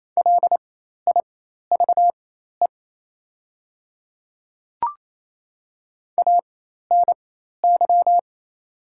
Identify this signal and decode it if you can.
Morse code, heard as a keyed tone, about 27 wpm, spaced as for 13 wpm, 700 Hz. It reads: LIVE ANY
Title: 4x Character Spacing - Top 200 Words - 28wpm